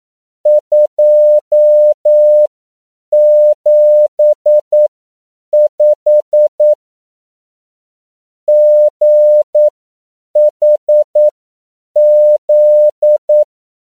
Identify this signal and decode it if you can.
Morse code, heard as a keyed tone, about 9 wpm, spaced as for 7 wpm, 600 Hz. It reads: 275 GHZ
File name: CW / 10PPM